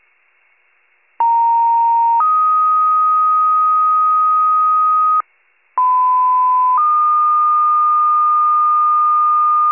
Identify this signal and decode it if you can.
Two-tone page, heard as a one-second tone, then a three-second tone, about 930 then 1250 Hz, about 1010 then 1220 Hz. It is heard from 1.2 to 9.7 s.